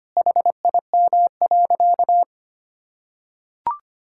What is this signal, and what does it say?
Morse code, heard as a keyed tone, about 25 wpm, 700 Hz. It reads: HIM.
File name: practice / 25wpm / Sentences with 4-Word Phrases - 25wpm